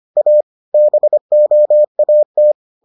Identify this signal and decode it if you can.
Morse code, heard as a keyed tone, about 25 wpm, 600 Hz. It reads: A BOAT